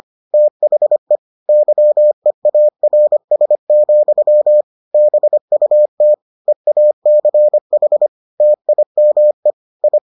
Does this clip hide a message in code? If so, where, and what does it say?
Yes: Morse code from 0.3 to 10.0 s, heard as a keyed tone, about 25 wpm, 600 Hz. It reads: THE YEARS, BUT EACH TIME I